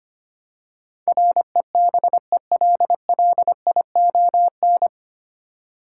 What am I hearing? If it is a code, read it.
Morse code, heard as a keyed tone, about 25 wpm, 690 Hz. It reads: REBELLION
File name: practice / 25wpm / 9 Letter Words - First 3 Letters Given - 25wpm